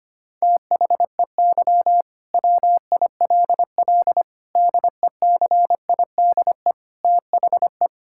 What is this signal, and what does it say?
Morse code, heard as a keyed tone, about 25 wpm, 700 Hz. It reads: THEY WILL DECIDE THE
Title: Sentences from Top 1600 Words 25wpm